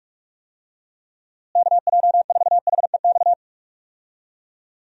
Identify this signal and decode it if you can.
Morse code, heard as a keyed tone, about 41 wpm, 700 Hz. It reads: KJ4HEX
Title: USA Call Signs - 6 Characters - Repeated 2x - 45wpm